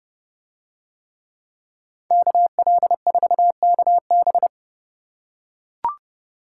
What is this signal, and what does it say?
Morse code, heard as a keyed tone, about 27 wpm, 700 Hz. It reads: KL4KB